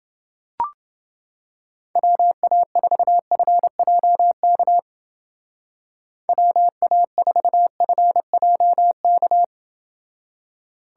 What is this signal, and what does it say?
Morse code, heard as a keyed tone, about 29 wpm, 700 Hz. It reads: WA4FJK WA4FJK